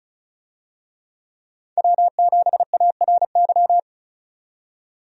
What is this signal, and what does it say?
Morse code, heard as a keyed tone, about 36 wpm, 700 Hz. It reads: W7ARY
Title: USA Call Signs - 5 Characters - 35wpm